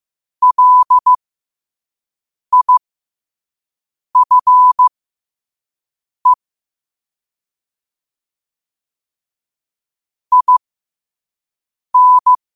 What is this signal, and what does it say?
Morse code, heard as a keyed tone, about 15 wpm, spaced as for 5 wpm, 1000 Hz. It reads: LIFE IN
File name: qotdfiles / 5wpm